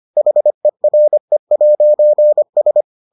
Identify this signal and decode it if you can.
Morse code, heard as a keyed tone, about 25 wpm, 600 Hz. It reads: HERE'S